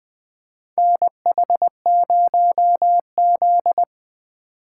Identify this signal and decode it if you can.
Morse code, heard as a keyed tone, about 20 wpm, 700 Hz. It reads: NH0Z
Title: USA Call Signs - 4 Characters - 4x Character Spacing - 20wpm